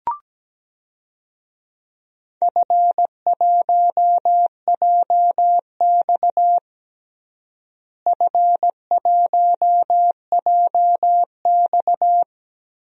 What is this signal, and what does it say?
Morse code, heard as a keyed tone, about 17 wpm, 700 Hz. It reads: F1JX F1JX
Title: DXCC-100 Call Signs - 4 Characters - Repeated 3x - 17wpm